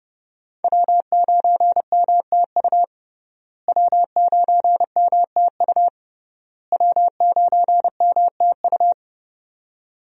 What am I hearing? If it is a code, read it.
Morse code, heard as a keyed tone, about 30 wpm, 700 Hz. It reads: W9MTU W9MTU W9MTU
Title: USA Call Signs - 5 Characters - Repeated 3x - 30wpm